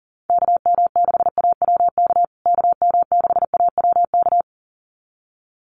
Morse code, heard as a keyed tone, about 40 wpm, 700 Hz. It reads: KM6AWK KM6AWK
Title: USA Call Signs Repeated 2x 40wpm